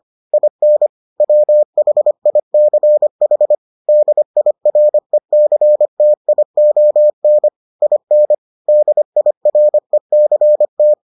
Morse code, heard as a keyed tone, about 25 wpm, 600 Hz. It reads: IN WHICH DIRECTION IN DIRECT